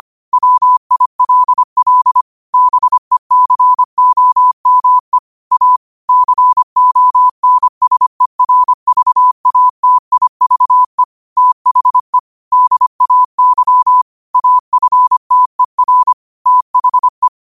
Morse code, heard as a keyed tone, about 25 wpm, 1000 Hz. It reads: WILL BECOME A CONSERVATIVE THE DAY AFTER THE
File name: qotdfiles / 25wpm